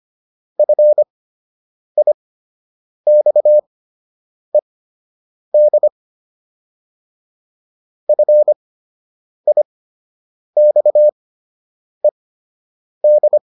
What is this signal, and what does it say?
Morse code, heard as a keyed tone, about 25 wpm, spaced as for 8 wpm, 600 Hz. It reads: FIXED FIXED